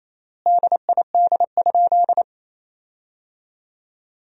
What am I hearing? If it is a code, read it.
Morse code, heard as a keyed tone, about 28 wpm, 700 Hz. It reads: DID?
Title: 4x Word-Spacing - Sentences from Top 100 Words - 28wpm